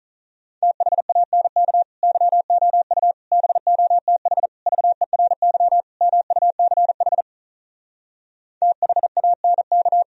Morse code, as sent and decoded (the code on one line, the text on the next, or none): - .... .- -. -.- / -.-- --- ..- / -... --- - .... / ...- . .-. -.-- / -- ..- -.-. .... / - .... .- -. -.-
THANK YOU BOTH VERY MUCH THANK